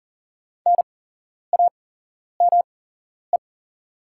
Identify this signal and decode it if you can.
Morse code, heard as a keyed tone, about 38 wpm, spaced as for 11 wpm, 700 Hz. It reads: NAME
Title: 8x Character Spacing - Top 200 Words - 40wpm